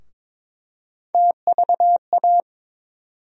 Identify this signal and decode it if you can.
Morse code, heard as a keyed tone, about 22 wpm, 700 Hz. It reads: TVA
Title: USA Call Sign Postfixes - 3 Letters 22wpm